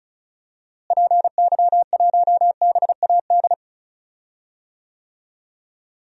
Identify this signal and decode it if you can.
Morse code, heard as a keyed tone, about 36 wpm, 700 Hz. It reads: PY1BAD
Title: DXCC-100 Call Signs - 6 Characters - Repeated 3x - 35wpm